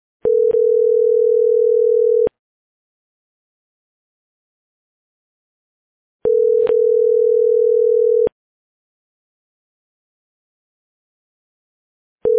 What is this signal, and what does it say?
Ringback tone (North America), 440 + 480 Hz.